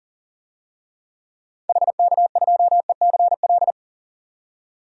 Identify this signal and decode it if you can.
Morse code, heard as a keyed tone, about 40 wpm, 700 Hz. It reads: HK2ECL
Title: DXCC-100 Call Signs - 6 Characters - 40wpm